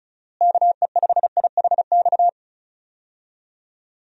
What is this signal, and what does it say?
Morse code, heard as a keyed tone, about 36 wpm, 700 Hz. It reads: KE5IHX
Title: USA Call Signs - 6 Characters - Repeated 3x - 35wpm